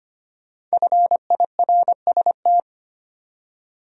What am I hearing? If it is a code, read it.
Morse code, heard as a keyed tone, about 25 wpm, 700 Hz. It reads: FIRST
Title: Top 100 Words 25wpm